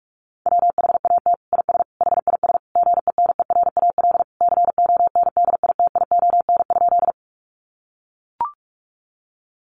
Morse code, heard as a keyed tone, about 44 wpm, 700 Hz. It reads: WHAT IS HIS GENERAL CONDITION?
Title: Sentences from Top 1000 Words 45wpm